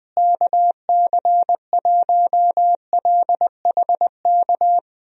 Morse code, heard as a keyed tone, about 20 wpm, 700 Hz. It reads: KC1LHK
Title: USA Call Signs - 6 Characters - Repeated 3x - 20wpm